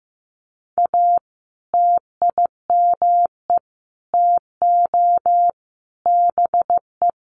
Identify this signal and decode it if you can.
Morse code, heard as a keyed tone, about 15 wpm, 700 Hz. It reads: A TIME TO BE
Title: Sentences from Top 1300 Words 15wpm